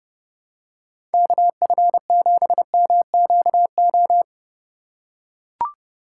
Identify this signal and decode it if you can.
Morse code, heard as a keyed tone, about 29 wpm, 700 Hz. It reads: KF7MQO
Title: USA Call Signs - 6 Characters - 3x Character Spacing - 30wpm